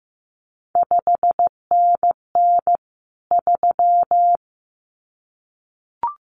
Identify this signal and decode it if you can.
Morse code, heard as a keyed tone, about 15 wpm, 700 Hz. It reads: 5NN 3
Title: CQ WW DX Contest Exchange Encore - 15wpm